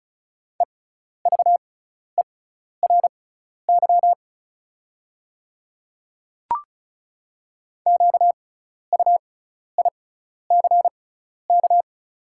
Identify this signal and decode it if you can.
Morse code, heard as a keyed tone, about 34 wpm, spaced as for 12 wpm, 690 Hz. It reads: EVERY QUICK
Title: 6x Character Spacing - Top 500 Words - 35wpm